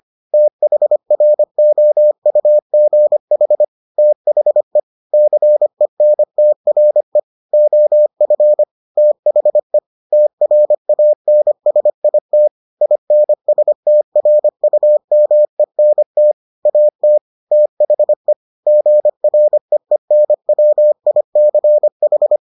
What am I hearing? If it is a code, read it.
Morse code, heard as a keyed tone, about 25 wpm, 600 Hz. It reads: THROUGH THE CENTRE OF THE TRANSIT INSTRUMENT AT THE GREENWICH